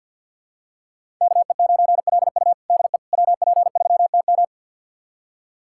Morse code, heard as a keyed tone, about 51 wpm, 700 Hz. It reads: KE9LU DE WP3TM
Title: USA Callsign DE Callsign - 5 Characters - 50wpm